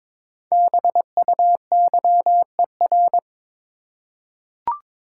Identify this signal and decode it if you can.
Morse code, heard as a keyed tone, about 22 wpm, 700 Hz. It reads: BUYER
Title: Five Letter Words - 22wpm